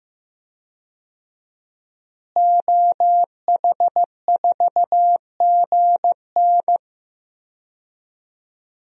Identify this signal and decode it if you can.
Morse code, heard as a keyed tone, about 16 wpm, 700 Hz. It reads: OH4GN
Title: DXCC-100 Call Signs - 5 Characters - Repeated 3x - 15wpm